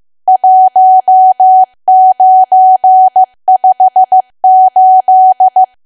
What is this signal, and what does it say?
Morse code, heard as a keyed tone, about 15 wpm, 750 Hz. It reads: 1958
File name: cwa / W1AW-Level2 / 15WPM